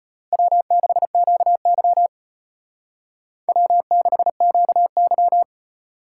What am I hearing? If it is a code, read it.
Morse code, heard as a keyed tone, about 30 wpm, 700 Hz. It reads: W6QY W6QY
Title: Speed Racing - USA 1x2 Call Signs 25wpm